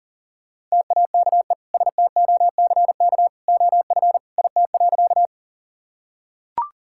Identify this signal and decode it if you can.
Morse code, heard as a keyed tone, about 40 wpm, 700 Hz. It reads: TAKE STOCK OF IT.
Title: Sentences from Top 1000 Words - Review Part 5 - 40wpm